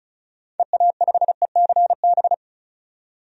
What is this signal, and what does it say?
Morse code, heard as a keyed tone, about 36 wpm, 690 Hz. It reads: EA5ECB
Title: DXCC-100 Call Signs - 6 Characters - 35wpm